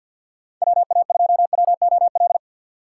Morse code, heard as a keyed tone, about 63 wpm, 700 Hz. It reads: WA2WOL